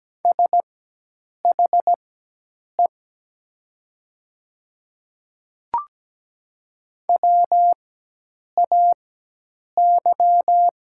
Morse code, heard as a keyed tone, about 17 wpm, spaced as for 8 wpm, 700 Hz. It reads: SHE WAY
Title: 4x Character Spacing - Top 100 Words - 17wpm